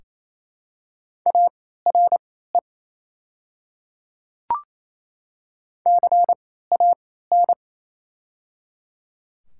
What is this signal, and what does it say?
Morse code, heard as a keyed tone, about 28 wpm, spaced as for 16 wpm, 700 Hz. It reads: ARE CAN
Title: 3x Character Spacing - Top 500 Words - 28wpm